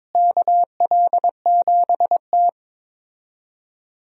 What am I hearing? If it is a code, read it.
Morse code, heard as a keyed tone, about 22 wpm, 700 Hz. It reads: KL7T
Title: USA Call Signs - 4 Characters - Repeated 2x - 22wpm